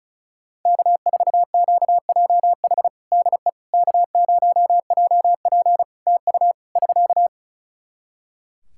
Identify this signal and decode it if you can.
Morse code, heard as a keyed tone, about 36 wpm, 700 Hz. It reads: K4QJH DE K0JP TU <SK>
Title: Practice QSO - Exchange 4 - Ending - 35wpm